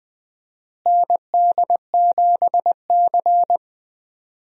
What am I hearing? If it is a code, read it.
Morse code, heard as a keyed tone, about 20 wpm, 700 Hz. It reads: ND7C
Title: USA Call Signs - 4 Characters - Repeated 3x - 20wpm